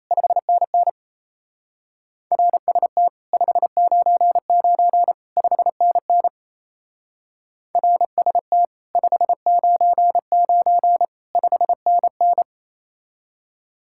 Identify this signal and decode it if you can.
Morse code, heard as a keyed tone, about 35 wpm, 700 Hz. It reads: 5NN RST 599 5NN RST 599 5NN